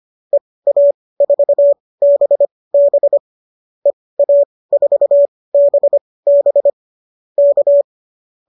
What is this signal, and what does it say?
Morse code, heard as a keyed tone, about 25 wpm, spaced as for 18 wpm, 590 Hz. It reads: EA4BB EA4BB K
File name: qso215 18